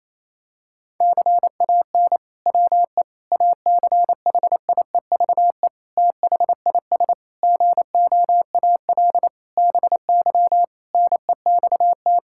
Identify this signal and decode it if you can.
Morse code, heard as a keyed tone, about 27 wpm, 700 Hz. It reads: CAN WE ACHIEVE THIS GOAL BY NEXT
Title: Sentences from Top 1000 Words - Review Part 5 - 28wpm